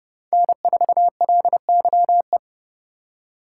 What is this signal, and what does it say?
Morse code, heard as a keyed tone, about 29 wpm, 700 Hz. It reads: N4LYE